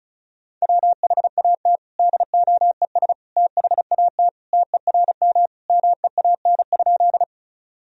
Morse code, heard as a keyed tone, about 34 wpm, 700 Hz. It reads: WHAT DOES THAT TERM MEAN?
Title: Sentences from Top 1000 Words - Review Part 4 - 35wpm